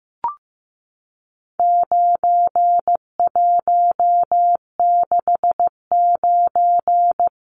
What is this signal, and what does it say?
Morse code, heard as a keyed tone, about 15 wpm, 700 Hz. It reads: 9169
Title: Numbers - 4-Digit 15wpm